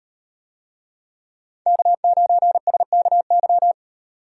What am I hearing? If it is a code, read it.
Morse code, heard as a keyed tone, about 38 wpm, 700 Hz. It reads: K9SKY